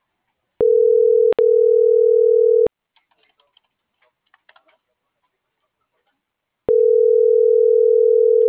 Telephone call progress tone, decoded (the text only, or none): ringback tone